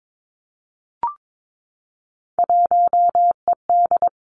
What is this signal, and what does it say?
Morse code, heard as a keyed tone, about 22 wpm, 700 Hz. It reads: 1ED
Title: Three Challenging Letter-Number Combinations 22wpm